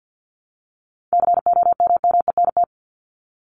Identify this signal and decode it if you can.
Morse code, heard as a keyed tone, about 50 wpm, 700 Hz. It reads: COMMENT
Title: Top 1751-2000 Words 50wpm